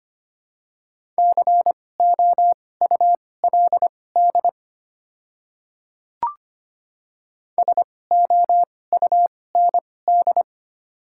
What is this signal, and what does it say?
Morse code, heard as a keyed tone, about 25 wpm, spaced as for 18 wpm, 690 Hz. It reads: COULD SOUND
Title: 2x Character Spacing - Top 100 Words - 25wpm